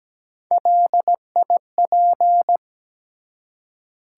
Morse code, heard as a keyed tone, about 17 wpm, 700 Hz. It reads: LIP
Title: Three Letter Words - 17wpm